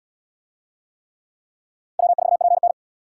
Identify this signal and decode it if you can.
Morse code, heard as a keyed tone, about 75 wpm, 700 Hz. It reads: W4PN